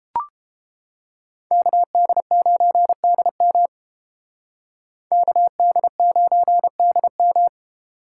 Morse code, heard as a keyed tone, about 29 wpm, 700 Hz. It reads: KD9DM KD9DM